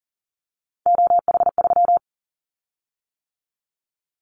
Morse code, heard as a keyed tone, about 40 wpm, 700 Hz. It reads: OH3
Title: Callsign Prefixes for Top 50 DXCC - 40wpm